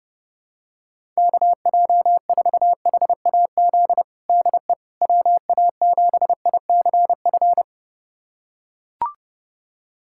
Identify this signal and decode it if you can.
Morse code, heard as a keyed tone, about 32 wpm, 700 Hz. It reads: KJ4HAZ DE WA7ICF
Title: USA Callsign DE Callsign - 6 Characters - 30wpm